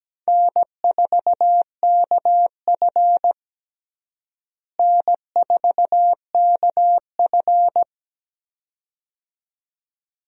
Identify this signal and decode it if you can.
Morse code, heard as a keyed tone, about 17 wpm, 700 Hz. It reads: N4KF N4KF
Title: USA Call Signs - 4 Characters - Repeated 2x - 17wpm